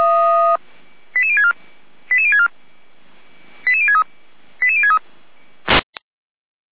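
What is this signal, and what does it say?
Five-tone selective call (ZVEI), heard as quick runs of five tones, tones of about 70 ms. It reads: 88064 88064 88063 88063
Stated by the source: ZVEI five-tone selective call received on SDR.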